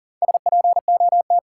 Morse code, heard as a keyed tone, about 40 wpm, 700 Hz. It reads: SPOT
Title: Sentences from Top 1100 Words - New Word Drill - 40wpm